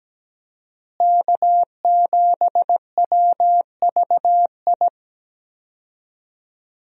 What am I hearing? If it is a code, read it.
Morse code, heard as a keyed tone, about 17 wpm, 700 Hz. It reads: K7WVI
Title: USA Call Signs - 5 Characters - Repeated 2x - 17wpm